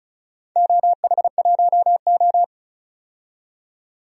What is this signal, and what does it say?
Morse code, heard as a keyed tone, about 34 wpm, 700 Hz. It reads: OH1O
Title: DXCC-100 Call Signs - 4 Characters - 35wpm